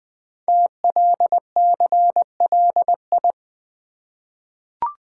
Morse code, heard as a keyed tone, about 20 wpm, 700 Hz. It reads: TLCLI
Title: Code Groups - 5 Letter:Numbers - 20wpm